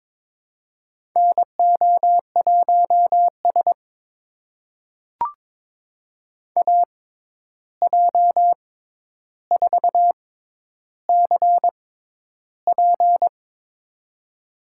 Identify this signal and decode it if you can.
Morse code, heard as a keyed tone, about 22 wpm, spaced as for 11 wpm, 700 Hz. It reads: NO1S AJ4CP
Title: USA Call Signs - 6x Character Spacing - 22wpm